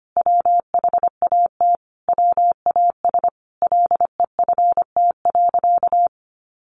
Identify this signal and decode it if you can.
Morse code, heard as a keyed tone, about 25 wpm, 700 Hz. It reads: WHAT WAS LEFT.
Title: Sentences from Top 1200 Words 25wpm